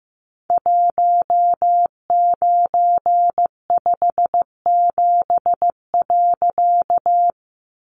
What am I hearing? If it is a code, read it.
Morse code, heard as a keyed tone, about 15 wpm, 700 Hz. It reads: 1957.
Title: A Sound Of Thunder 15wpm